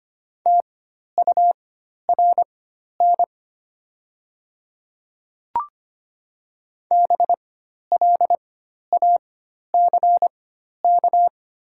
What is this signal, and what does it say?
Morse code, heard as a keyed tone, about 25 wpm, spaced as for 12 wpm, 700 Hz. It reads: TURN BLACK
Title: 4x Character Spacing - Top 500 Words - 25wpm